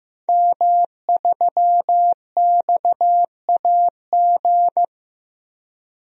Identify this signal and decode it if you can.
Morse code, heard as a keyed tone, about 15 wpm, 700 Hz. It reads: M3XAG
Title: DXCC-100 Call Signs - 5 Characters - Repeated 3x - 15wpm